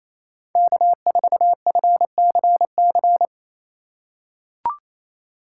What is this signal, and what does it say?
Morse code, heard as a keyed tone, about 28 wpm, 700 Hz. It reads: K4FCC
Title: K1USN SST QSO Practice - 28wpm